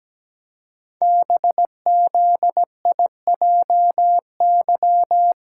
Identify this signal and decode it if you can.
Morse code, heard as a keyed tone, about 17 wpm, 700 Hz. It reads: BZIJY
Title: Code Groups - 5 Letters - 17wpm